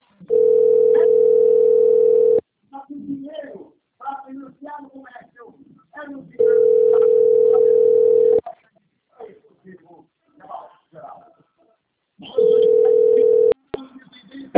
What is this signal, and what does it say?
Ringback tone (North America), 440 + 480 Hz, with speech.
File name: rg-1600-unknown- - - .46512